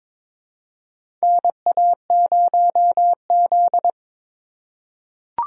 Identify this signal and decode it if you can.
Morse code, heard as a keyed tone, about 22 wpm, 700 Hz. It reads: NA0Z
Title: USA Call Signs - 4 Characters - 3x Character Spacing - 22wpm